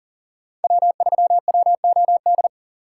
Morse code, heard as a keyed tone, about 40 wpm, 700 Hz. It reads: W3WOD
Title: USA Call Signs - 5 Characters - 2x Character Spacing - 40wpm